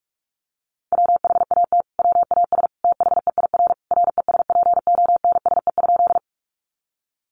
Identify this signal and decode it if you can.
Morse code, heard as a keyed tone, about 44 wpm, 700 Hz. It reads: WHAT WAS THEIR RESPONSE?
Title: Sentences from Top 1400 Words 45wpm